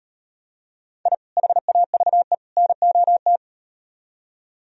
Morse code, heard as a keyed tone, about 36 wpm, 700 Hz. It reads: I HAVE NOT